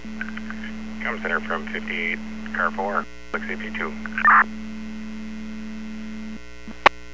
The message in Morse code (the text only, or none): none